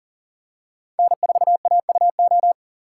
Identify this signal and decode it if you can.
Morse code, heard as a keyed tone, about 40 wpm, 700 Hz. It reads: N4AUO
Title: USA Call Signs - 5 Characters - 2x Character Spacing - 40wpm